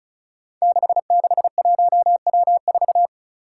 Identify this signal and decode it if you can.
Morse code, heard as a keyed tone, about 36 wpm, 700 Hz. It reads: 661W4
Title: Five Challenging Letter-Number Combinations 35wpm